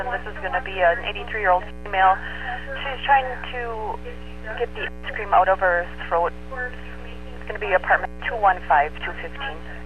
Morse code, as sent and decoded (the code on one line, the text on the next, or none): none